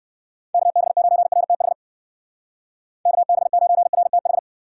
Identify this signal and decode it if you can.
Morse code, heard as a keyed tone, about 59 wpm, 700 Hz. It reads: KB9RTH KB9RTH